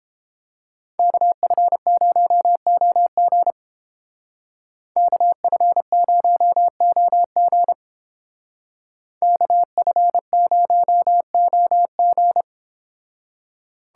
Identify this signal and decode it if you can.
Morse code, heard as a keyed tone, about 30 wpm, 700 Hz. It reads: KF0OG KF0OG KF0OG